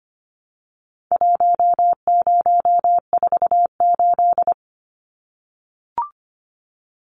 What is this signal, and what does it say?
Morse code, heard as a keyed tone, about 25 wpm, 700 Hz. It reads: 1048
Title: Numbers - Contest Style - 4-Digit 25wpm